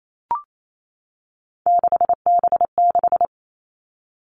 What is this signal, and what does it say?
Morse code, heard as a keyed tone, about 28 wpm, 700 Hz. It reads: 6B6
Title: All Cut Numbers - 3-Digit 28wpm